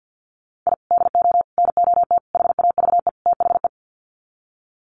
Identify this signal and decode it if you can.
Morse code, heard as a keyed tone, about 50 wpm, 700 Hz. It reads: I DO NOT HAVE THE
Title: Sets of 5 Words 50wpm